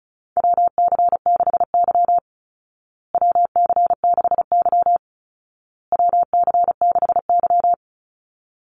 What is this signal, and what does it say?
Morse code, heard as a keyed tone, about 34 wpm, 700 Hz. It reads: WC6Y WC6Y WC6Y
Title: Call Signs Repeated 2x - 35wpm